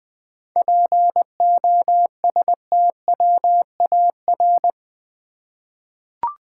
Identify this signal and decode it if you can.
Morse code, heard as a keyed tone, about 20 wpm, 700 Hz. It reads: POSTWAR
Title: 7 Letter Words - First 3 Letters Given - 20wpm